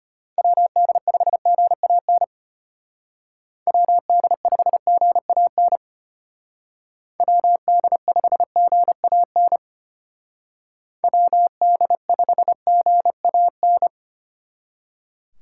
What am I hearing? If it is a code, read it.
Morse code, heard as a keyed tone, about 33 wpm, 700 Hz. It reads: WD5GAN WD5GAN WD5GAN WD5GAN